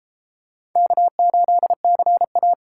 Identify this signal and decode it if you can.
Morse code, heard as a keyed tone, about 32 wpm, 700 Hz. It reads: K8CA